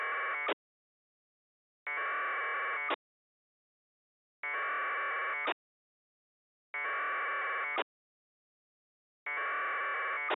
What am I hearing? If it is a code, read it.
Packet radio (AX.25, as in APRS).